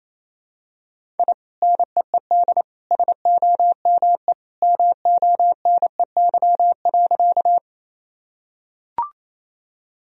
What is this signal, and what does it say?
Morse code, heard as a keyed tone, about 28 wpm, 700 Hz. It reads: I NEED SOME MONEY.
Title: Sentences from Top 1000 Words - Review Part 1 - 28wpm